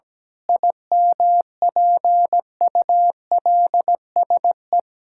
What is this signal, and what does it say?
Morse code, heard as a keyed tone, about 17 wpm, 700 Hz. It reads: IMPULSE
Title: Pangram Challenge v2 - 17wpm